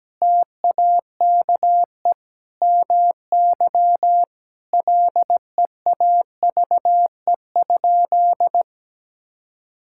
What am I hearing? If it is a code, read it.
Morse code, heard as a keyed tone, about 17 wpm, 700 Hz. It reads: TAKE MY LEAVE?
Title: Sentences from Top 1000 Words - Review Part 5 - 17wpm